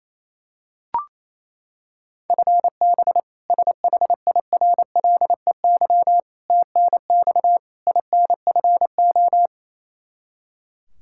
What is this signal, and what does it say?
Morse code, heard as a keyed tone, about 28 wpm, 700 Hz. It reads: FB SHIRLEY TNX INFO
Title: K1USN SST QSO Practice 28wpm